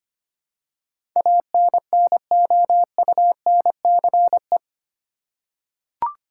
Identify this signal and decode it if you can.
Morse code, heard as a keyed tone, about 25 wpm, 700 Hz. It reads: ANNOUNCE
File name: Eight Letter Words - 25wpm